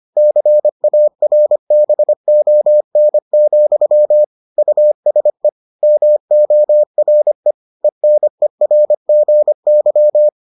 Morse code, heard as a keyed tone, about 25 wpm, 600 Hz. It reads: CARBON, USE MORE ENERGY